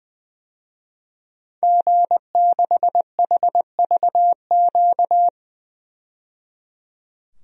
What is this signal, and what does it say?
Morse code, heard as a keyed tone, about 20 wpm, 700 Hz. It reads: G6HVQ
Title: DXCC-100 Call Signs - 5 Characters - 20wpm